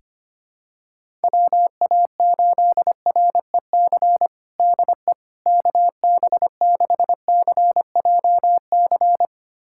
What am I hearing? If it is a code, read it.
Morse code, heard as a keyed tone, about 25 wpm, 700 Hz. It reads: WA8REC DE KB6CJC
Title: USA Callsign DE Callsign - 6 Characters - 25wpm